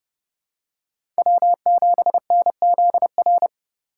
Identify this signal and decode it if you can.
Morse code, heard as a keyed tone, about 30 wpm, 700 Hz. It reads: W7NZR
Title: USA Call Signs - 5 Characters - 4x Character Spacing - 30wpm